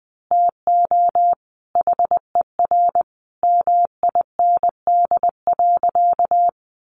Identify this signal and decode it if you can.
Morse code, heard as a keyed tone, about 20 wpm, 700 Hz. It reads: TO HER MIND.
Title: Sentences from Top 1300 Words 20wpm